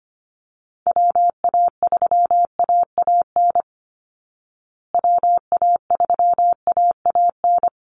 Morse code, heard as a keyed tone, about 25 wpm, 700 Hz. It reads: WA3AAN WA3AAN
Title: Call Signs Repeated 2x - 25wpm